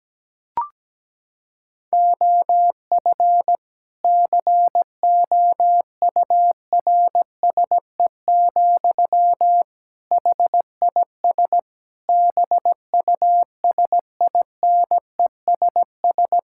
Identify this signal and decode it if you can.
Morse code, heard as a keyed tone, about 17 wpm, 700 Hz. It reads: OF COURSE, HIS BUSINESS
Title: Sentences from Top 1000 Words - Review Part 5 - 17wpm